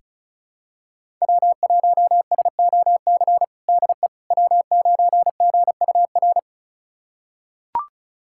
Morse code, heard as a keyed tone, about 36 wpm, 700 Hz. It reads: W1SOC DE W9GUR
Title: USA Callsign DE Callsign - 5 Characters - 35wpm